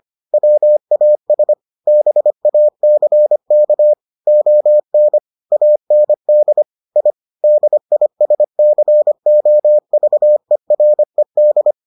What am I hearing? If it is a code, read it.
Morse code, heard as a keyed tone, about 25 wpm, 600 Hz. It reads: WAS BACK ON AND I DISCOVERED